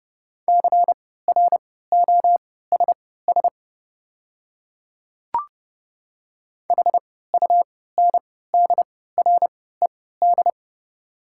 Morse code, heard as a keyed tone, about 30 wpm, spaced as for 17 wpm, 700 Hz. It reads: CROSS HUNDRED
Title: 3x Character Spacing - Top 500 Words - 30wpm